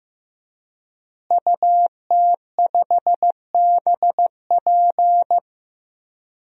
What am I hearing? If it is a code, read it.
Morse code, heard as a keyed tone, about 15 wpm, 700 Hz. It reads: UT5BP
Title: DXCC-100 Call Signs - 5 Characters - Repeated 3x - 15wpm